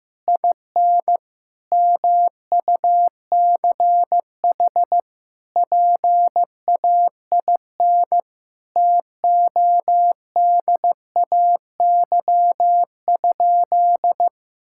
Morse code, heard as a keyed tone, about 16 wpm, 700 Hz. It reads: IN MUCH PAIN TODAY?
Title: Sentences from Top 1600 Words 15wpm